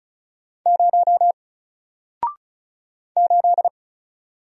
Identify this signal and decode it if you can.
Morse code, heard as a keyed tone, about 36 wpm, 700 Hz. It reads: 0 8
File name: Single-Letter-Number-35wpm